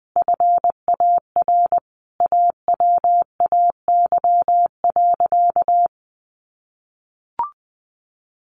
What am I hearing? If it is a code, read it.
Morse code, heard as a keyed tone, about 20 wpm, 700 Hz. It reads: FAR AWAY.
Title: Sentences from Top 1500 Words 20wpm